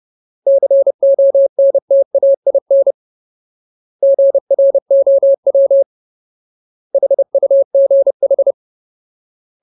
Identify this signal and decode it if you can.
Morse code, heard as a keyed tone, about 30 wpm, 550 Hz. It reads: CONTAIN GROW HUGH
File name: Ultimate Word List / 30WPM Ultimate